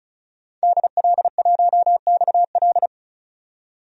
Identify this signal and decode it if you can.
Morse code, heard as a keyed tone, about 35 wpm, 700 Hz. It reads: DL1XL
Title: DXCC-100 Call Signs - 5 Characters - 35wpm